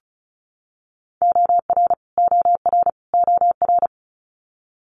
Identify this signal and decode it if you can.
Morse code, heard as a keyed tone, about 36 wpm, 700 Hz. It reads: OR OR OR
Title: Multi-letter Phonemes Repeated 3x 35wpm